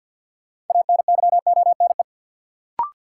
Morse code, heard as a keyed tone, about 47 wpm, 700 Hz. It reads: ANYONE
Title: Sentences from Top 900 Words - New Word Drill - 50wpm